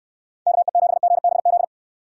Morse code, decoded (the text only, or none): KE7GBZ